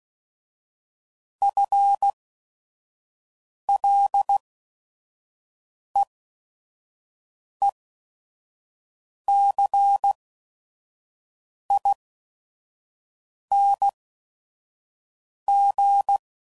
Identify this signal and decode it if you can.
Morse code, heard as a keyed tone, about 16 wpm, spaced as for 5 wpm, 800 Hz. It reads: FLEECING